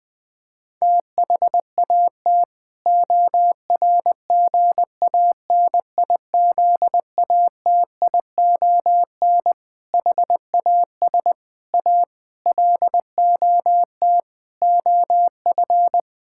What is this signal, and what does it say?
Morse code, heard as a keyed tone, about 20 wpm, 700 Hz. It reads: THAT ORGANIZATION HAS A LOT OF
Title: Sentences from Top 1000 Words - Review Part 6 - 20wpm